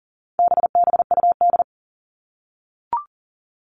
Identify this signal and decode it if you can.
Morse code, heard as a keyed tone, about 40 wpm, 700 Hz. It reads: BBUD
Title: All Cut Numbers - 4-Digit 40wpm